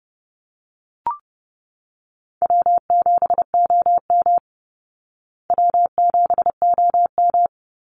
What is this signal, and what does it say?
Morse code, heard as a keyed tone, about 30 wpm, 700 Hz. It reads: W7OM W7OM
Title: Easy Call Signs Repeated 3x - 30wpm